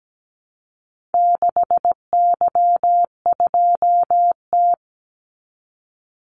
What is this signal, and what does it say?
Morse code, heard as a keyed tone, about 17 wpm, 700 Hz. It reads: 6Y2T
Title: Call Signs Repeated 3x - 17wpm